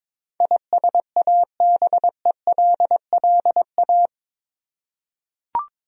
Morse code, heard as a keyed tone, about 21 wpm, 700 Hz. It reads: ISABELLA
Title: USA Common Names - 22wpm